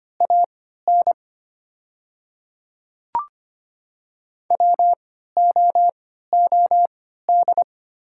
Morse code, heard as a keyed tone, about 25 wpm, spaced as for 14 wpm, 700 Hz. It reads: AN WOOD
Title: 3x Character Spacing - Top 300 Words - 25wpm